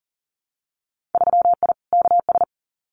Morse code, heard as a keyed tone, about 40 wpm, 700 Hz. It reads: 3I KS
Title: Winter Field Day Exchanges 40wpm